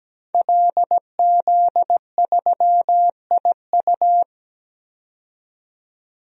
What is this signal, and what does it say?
Morse code, heard as a keyed tone, about 17 wpm, 700 Hz. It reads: LZ3IU
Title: DXCC-100 Call Signs - 5 Characters - 17wpm